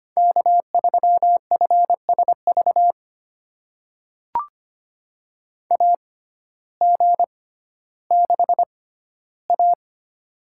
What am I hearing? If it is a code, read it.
Morse code, heard as a keyed tone, about 25 wpm, spaced as for 14 wpm, 700 Hz. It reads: K3FSV AG6A
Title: USA Call Signs - 6x Character Spacing - 25wpm